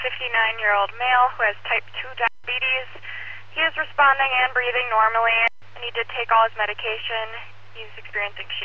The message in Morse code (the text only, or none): none